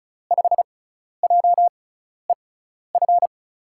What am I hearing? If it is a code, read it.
Morse code, heard as a keyed tone, about 36 wpm, spaced as for 12 wpm, 700 Hz. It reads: 5JEF
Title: USA Call Signs - 6 Characters - 6x Character Spacing - 35wpm